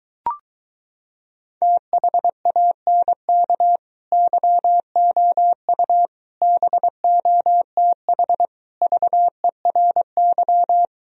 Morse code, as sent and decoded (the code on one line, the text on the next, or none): - .... .- -. -.- / -.-- --- ..- / -... --- - .... / ...- . .-. -.--
THANK YOU BOTH VERY